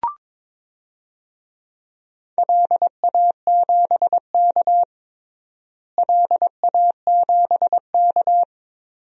Morse code, heard as a keyed tone, about 22 wpm, 700 Hz. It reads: LA7K LA7K
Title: DXCC-100 Call Signs - 4 Characters - Repeated 3x - 22wpm